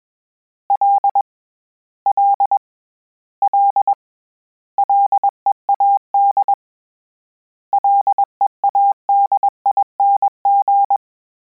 Morse code, heard as a keyed tone, about 21 wpm, 800 Hz. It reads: L L L LEAD LEADING
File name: Combinations / Combinations-mp3s / Combinations@20